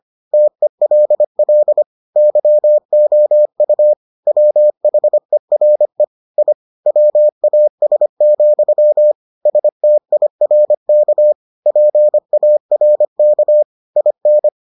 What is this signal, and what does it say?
Morse code, heard as a keyed tone, about 25 wpm, 600 Hz. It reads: TELL YOU WHERE I WAS, STIRK PARK IN